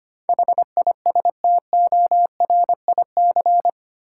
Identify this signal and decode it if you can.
Morse code, heard as a keyed tone, about 25 wpm, 700 Hz. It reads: HISTORIC
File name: Eight Letter Words - 25wpm